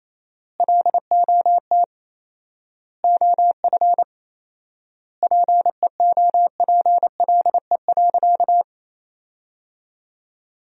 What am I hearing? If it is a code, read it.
Morse code, heard as a keyed tone, about 28 wpm, 700 Hz. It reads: LOT OF PEOPLE.
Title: 4x Word-Spacing - Sentences from Top 500 Words - 28wpm